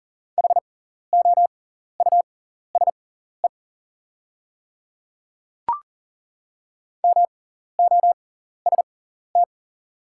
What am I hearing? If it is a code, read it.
Morse code, heard as a keyed tone, about 40 wpm, spaced as for 14 wpm, 690 Hz. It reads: HOUSE MOST
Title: 6x Character Spacing - Top 200 Words - 40wpm